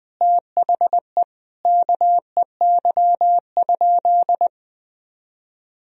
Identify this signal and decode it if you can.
Morse code, heard as a keyed tone, about 20 wpm, 700 Hz. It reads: THE KEY?
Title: Sentences from Top 1000 Words - Review Part 4 - 20wpm